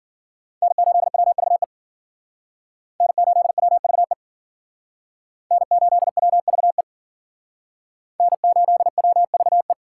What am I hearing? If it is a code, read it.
Morse code, heard as a keyed tone, about 52 wpm, 700 Hz. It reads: N8WVE N8WVE N8WVE N8WVE